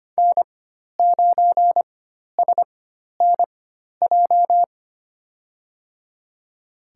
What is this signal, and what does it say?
Morse code, heard as a keyed tone, about 25 wpm, spaced as for 12 wpm, 700 Hz. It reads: N9SNJ
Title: USA Call Signs - 4x Character Spacing - 25wpm